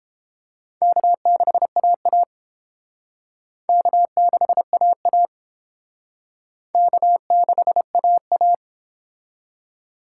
Morse code, heard as a keyed tone, about 29 wpm, 700 Hz. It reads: K6AA K6AA K6AA